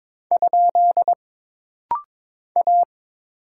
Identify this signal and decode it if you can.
Morse code, heard as a keyed tone, about 22 wpm, 700 Hz. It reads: ? A
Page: This rapid-fire timing is geared to push your ICR (Instant Character Recognition) skills.